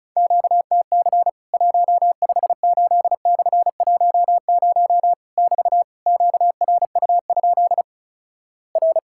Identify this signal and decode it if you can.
Morse code, heard as a keyed tone, about 36 wpm, 700 Hz. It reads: QTC 158/10 <BT> QRU? R
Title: Worked All Europe DX Contest - QTC Exchange - 35wpm